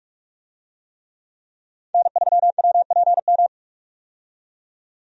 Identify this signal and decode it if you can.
Morse code, heard as a keyed tone, about 46 wpm, 700 Hz. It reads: N3WPM
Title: USA Call Signs - 5 Characters - 45wpm